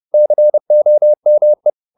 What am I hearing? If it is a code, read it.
Morse code, heard as a keyed tone, about 30 wpm, 600 Hz. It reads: COME
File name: Top-100-Words-30